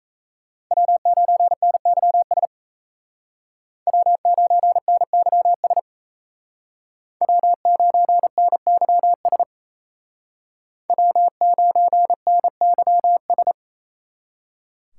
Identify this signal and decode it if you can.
Morse code, heard as a keyed tone, about 37 wpm, 700 Hz. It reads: W9NYS W9NYS W9NYS W9NYS